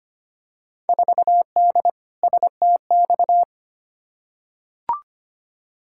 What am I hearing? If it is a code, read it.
Morse code, heard as a keyed tone, about 25 wpm, 700 Hz. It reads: 4D STX
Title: Field Day Exchanges Encore - 25wpm